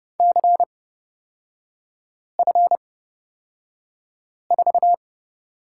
Morse code, heard as a keyed tone, about 30 wpm, spaced as for 5 wpm, 700 Hz. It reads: CF4
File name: Lesson 36e - BT - Three Characters 30wpm